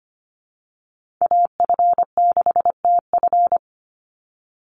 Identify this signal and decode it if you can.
Morse code, heard as a keyed tone, about 25 wpm, 700 Hz. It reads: AF6TF
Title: USA Call Signs 25wpm